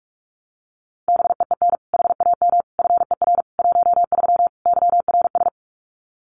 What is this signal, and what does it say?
Morse code, heard as a keyed tone, about 44 wpm, 700 Hz. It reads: BEEN HAM FER 13 YRS
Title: Common QSO Phrases 45wpm